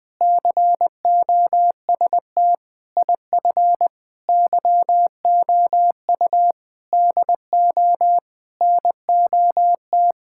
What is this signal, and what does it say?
Morse code, heard as a keyed tone, about 19 wpm, 700 Hz. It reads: COST IF YOU DO NOT
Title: Sentences with 8-Word Phrases - Spoken Beginning - 20wpm